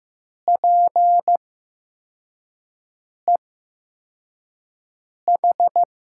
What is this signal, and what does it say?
Morse code, heard as a keyed tone, about 15 wpm, spaced as for 4 wpm, 700 Hz. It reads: PEH
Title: USA Call Signs - 5 Characters - 8x Character Spacing - 15wpm